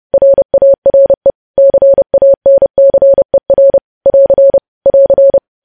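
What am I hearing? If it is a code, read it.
Morse code, heard as a keyed tone, about 30 wpm, 550 Hz. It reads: RARE CANCER <AR> <AR>
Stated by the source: News headlines sent in Morse code at 30.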